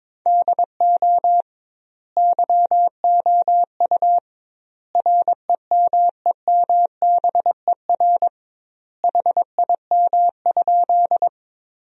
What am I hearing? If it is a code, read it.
Morse code, heard as a keyed tone, about 22 wpm, 700 Hz. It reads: DO YOU REMEMBER HIM?
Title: 2x Word-Spacing - Sentences from Top 400 Words - 22wpm